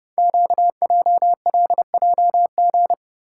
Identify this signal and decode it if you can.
Morse code, heard as a keyed tone, about 30 wpm, 690 Hz. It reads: QJLJG
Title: Code Groups - 5 Letters - 30wpm